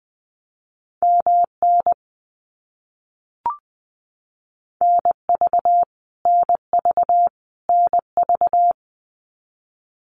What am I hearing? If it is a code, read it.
Morse code, heard as a keyed tone, about 20 wpm, 700 Hz. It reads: MN NV NV NV
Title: US States Repeated 3x 20wpm